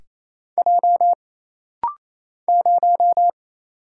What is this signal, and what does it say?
Morse code, heard as a keyed tone, about 28 wpm, 700 Hz. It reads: J 0
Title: Single Letter-Number - Rapid-Fire 28wpm